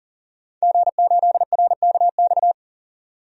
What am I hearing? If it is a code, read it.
Morse code, heard as a keyed tone, about 39 wpm, 700 Hz. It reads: G8RKX
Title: DXCC-100 Call Signs - 5 Characters - 40wpm